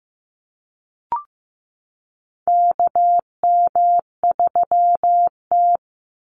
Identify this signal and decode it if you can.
Morse code, heard as a keyed tone, about 15 wpm, 700 Hz. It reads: KM3T
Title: Call Signs Repeated 2x - 15wpm